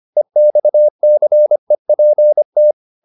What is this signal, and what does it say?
Morse code, heard as a keyed tone, about 25 wpm, 600 Hz. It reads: EXCEPT